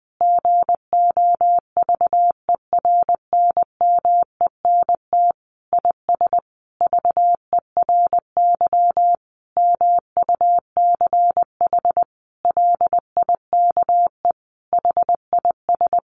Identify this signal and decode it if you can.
Morse code, heard as a keyed tone, about 19 wpm, 700 Hz. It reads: GOVERNMENT IS VERY MUCH LIKE HIS
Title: Sentences from Top 1300 Words 20wpm